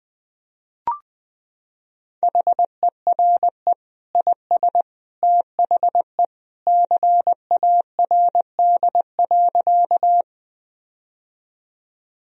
Morse code, heard as a keyed tone, about 20 wpm, 700 Hz. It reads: HERE IS THE CARD.
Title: Sentences from Top 1000 Words - Review Part 4 - 20wpm